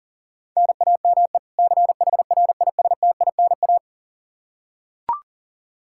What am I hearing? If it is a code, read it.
Morse code, heard as a keyed tone, about 40 wpm, 700 Hz. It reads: NAME CHRISTINA
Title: Practice QSO - Exchange 1 - Beginning - 40wpm